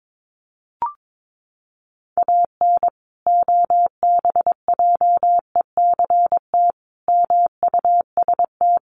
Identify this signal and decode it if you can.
Morse code, heard as a keyed tone, about 22 wpm, 700 Hz. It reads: AN OBJECT MUST
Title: Sentences from Top 1400 Words 22wpm